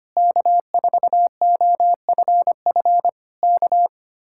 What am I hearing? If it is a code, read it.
Morse code, heard as a keyed tone, about 24 wpm, 700 Hz. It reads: K4OFF K
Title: Speed-Racing Common QSO Phrases - 25wpm